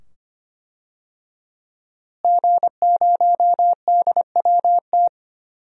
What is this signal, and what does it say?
Morse code, heard as a keyed tone, about 25 wpm, 700 Hz. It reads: G0DWT
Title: DXCC-100 Call Signs - 5 Characters - Repeated 2x - 25wpm